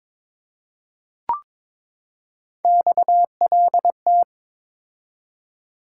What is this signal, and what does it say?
Morse code, heard as a keyed tone, about 22 wpm, 700 Hz. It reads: XLT
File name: practice / 22wpm / Three Letter-Number Combinations - 22wpm